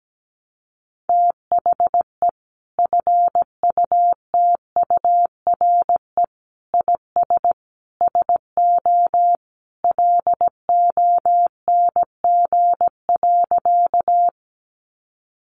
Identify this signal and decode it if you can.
Morse code, heard as a keyed tone, about 17 wpm, 700 Hz. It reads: THE FUTURE IS SO LONG.
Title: Sentences from Top 700 Words 17wpm